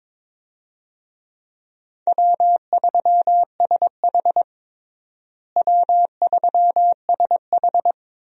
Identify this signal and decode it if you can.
Morse code, heard as a keyed tone, about 22 wpm, 700 Hz. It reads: W3SH W3SH
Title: USA Call Signs - 4 Characters - Repeated 3x - 22wpm